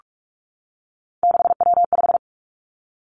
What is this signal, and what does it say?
Morse code, heard as a keyed tone, about 46 wpm, 700 Hz. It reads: 6W5
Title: Three Challenging Letter-Number Combinations 45wpm